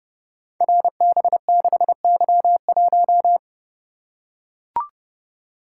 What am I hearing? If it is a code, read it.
Morse code, heard as a keyed tone, about 28 wpm, 700 Hz. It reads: RB6Y1
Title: Code Groups - 5 Letter:Numbers - 30wpm